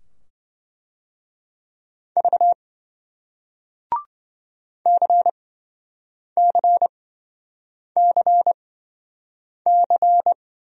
Morse code, heard as a keyed tone, about 27 wpm, spaced as for 7 wpm, 700 Hz. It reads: V CCCC